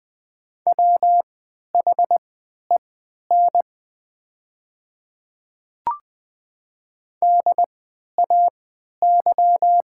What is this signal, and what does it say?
Morse code, heard as a keyed tone, about 20 wpm, spaced as for 11 wpm, 700 Hz. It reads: WHEN DAY
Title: 3x Character Spacing - Top 200 Words - 20wpm